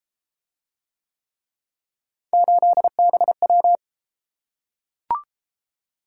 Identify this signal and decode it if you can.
Morse code, heard as a keyed tone, about 32 wpm, 700 Hz. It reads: 8BW